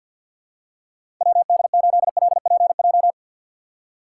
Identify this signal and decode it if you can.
Morse code, heard as a keyed tone, about 52 wpm, 700 Hz. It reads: WD8LPJ
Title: USA Call Signs - 6 Characters - 6x Character Spacing - 50wpm